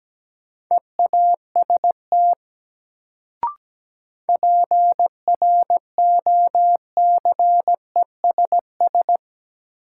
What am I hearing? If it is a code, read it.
Morse code, heard as a keyed tone, about 17 wpm, 700 Hz. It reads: EAST PROCESS
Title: Sets of 3 Words - New Word Drill - 17wpm